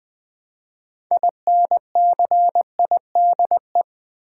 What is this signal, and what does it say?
Morse code, heard as a keyed tone, about 20 wpm, 700 Hz. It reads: INCIDE
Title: Build A Word - 2 Letters At A Time - 8-Letter Words - 20wpm